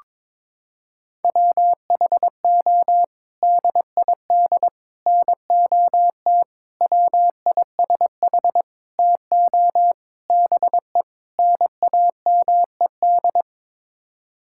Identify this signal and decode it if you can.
Morse code, heard as a keyed tone, about 22 wpm, 700 Hz. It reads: WHO DID NOT WISH TO BE NAMED
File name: practice / 22wpm / Sets of 7 Words - 22wpm